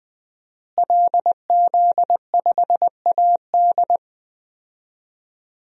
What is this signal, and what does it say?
Morse code, heard as a keyed tone, about 20 wpm, 700 Hz. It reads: LZ5AD
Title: DXCC-100 Call Signs - 5 Characters - Repeated 2x - 20wpm